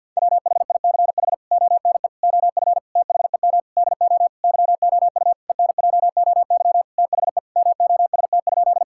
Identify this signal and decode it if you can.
Morse code, heard as a keyed tone, about 50 wpm, 700 Hz. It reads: WHICH ONE OF THEM DO YOU ENJOY THE MOST?
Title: Sentences from Top 1000 Words - Review Part 5 - 50wpm